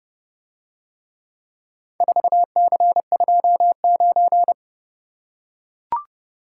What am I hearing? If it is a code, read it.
Morse code, heard as a keyed tone, about 29 wpm, 700 Hz. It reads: 4C29